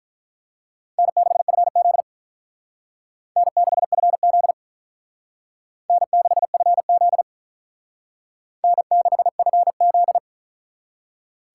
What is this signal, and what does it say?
Morse code, heard as a keyed tone, about 43 wpm, 700 Hz. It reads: N6FZ N6FZ N6FZ N6FZ
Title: Speed Racing - USA 1x2 Call Signs 35wpm